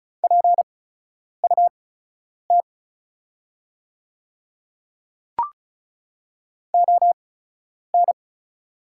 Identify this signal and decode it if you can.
Morse code, heard as a keyed tone, about 36 wpm, spaced as for 10 wpm, 700 Hz. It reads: PUT ON
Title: 8x Character Spacing - Top 200 Words - 35wpm